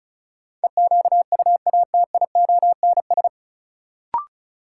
Morse code, heard as a keyed tone, about 36 wpm, 700 Hz. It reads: EQUATIONS
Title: Sets of 5 Words - New Word Drill - 35wpm